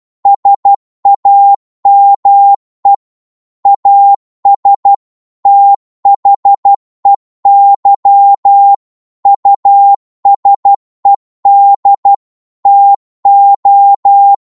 Morse code, heard as a keyed tone, about 12 wpm, 800 Hz. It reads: SAME AS THEY USED TO